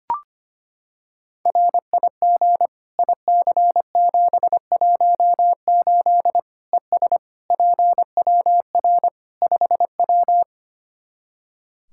Morse code, heard as a keyed tone, about 25 wpm, 700 Hz. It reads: RIG IC718 ES PWR 5W
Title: K1USN SST QSO Practice 25wpm